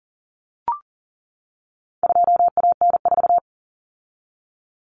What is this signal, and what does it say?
Morse code, heard as a keyed tone, about 40 wpm, 700 Hz. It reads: 2AN4
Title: Common Cut Numbers - 4-Digit 40wpm